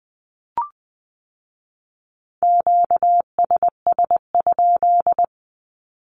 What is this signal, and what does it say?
Morse code, heard as a keyed tone, about 20 wpm, 700 Hz. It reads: QSS?
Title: Q-Codes with Spoken Meaning - 20wpm